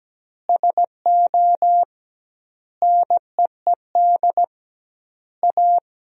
Morse code, heard as a keyed tone, about 17 wpm, 700 Hz. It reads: SO NEED A
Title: 2x Word-Spacing - Sentences from Top 200 Words - 17wpm